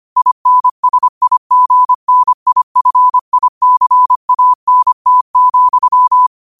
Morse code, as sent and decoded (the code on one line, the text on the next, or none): .. -. ... .. --. -. .. ..-. .. -.-. .- -. - --..--
INSIGNIFICANT,